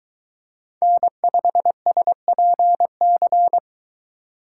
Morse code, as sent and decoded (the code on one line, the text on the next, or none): -. ..... ... .--. -.-.
N5SPC